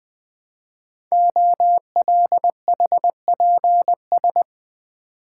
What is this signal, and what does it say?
Morse code, heard as a keyed tone, about 20 wpm, 700 Hz. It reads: OLHPS
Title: Code Groups - 5 Letters - 20wpm